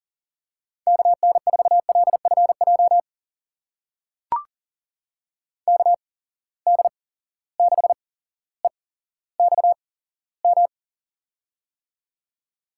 Morse code, heard as a keyed tone, about 38 wpm, spaced as for 17 wpm, 700 Hz. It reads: KN4LFJ KD6EXM
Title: USA Call Signs - 6 Characters - 8x Character Spacing - 40wpm